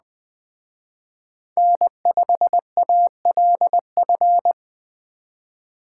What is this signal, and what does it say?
Morse code, heard as a keyed tone, about 20 wpm, 700 Hz. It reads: N5ALF
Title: Speed Racing - USA 1x3 Call Signs 17wpm